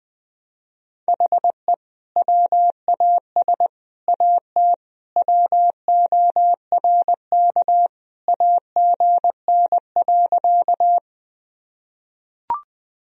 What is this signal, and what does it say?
Morse code, heard as a keyed tone, about 20 wpm, 700 Hz. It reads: HE WAS AT WORK AGN.
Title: Sentences from Top 200 Words and Abbreviations 20wpm